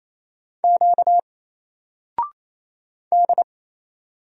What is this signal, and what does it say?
Morse code, heard as a keyed tone, about 28 wpm, 700 Hz. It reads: Q D
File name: Single-Letter-Number-28wpm